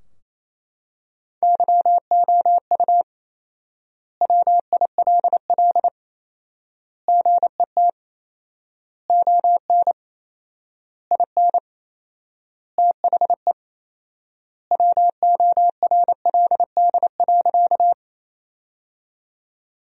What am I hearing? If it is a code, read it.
Morse code, heard as a keyed tone, about 28 wpm, 700 Hz. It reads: YOU WILL GET ON IN THE WORLD.
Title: 4x Word-Spacing - Sentences from Top 200 Words - 28wpm